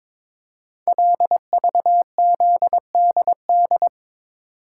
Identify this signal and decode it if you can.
Morse code, heard as a keyed tone, about 22 wpm, 700 Hz. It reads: LVZDD
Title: Code Groups - 5 Letters - 22wpm